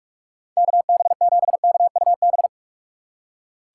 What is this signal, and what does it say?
Morse code, heard as a keyed tone, about 46 wpm, 700 Hz. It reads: KB7KUB
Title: USA Call Signs - 6 Characters - 3x Character Spacing - 45wpm